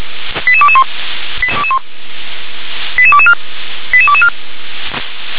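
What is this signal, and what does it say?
Five-tone selective call (ZVEI), heard as quick runs of five tones, tones of about 70 ms. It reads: 88221 88224 88224
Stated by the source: ZVEI five-tone selective call received via SDR.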